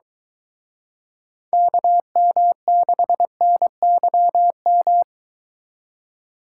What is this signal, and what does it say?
Morse code, heard as a keyed tone, about 23 wpm, 700 Hz. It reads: KM6NYM